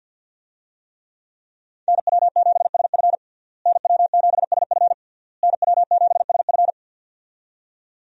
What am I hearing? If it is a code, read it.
Morse code, heard as a keyed tone, about 48 wpm, 700 Hz. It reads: NW7SF NW7SF NW7SF
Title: USA Call Signs - 5 Characters - Repeated 3x - 50wpm